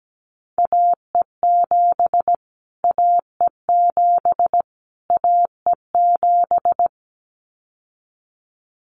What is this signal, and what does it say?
Morse code, heard as a keyed tone, about 17 wpm, 700 Hz. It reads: AE7 AE7 AE7
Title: Callsign Prefixes for Top 25 DXCC - Repeated 3x - 17wpm